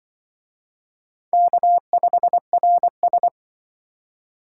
Morse code, heard as a keyed tone, about 25 wpm, 700 Hz. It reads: K5RS